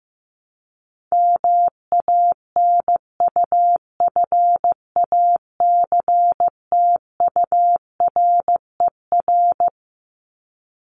Morse code, heard as a keyed tone, about 15 wpm, 700 Hz. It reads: MANUFACTURER
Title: Top 1751-2000 Words 15wpm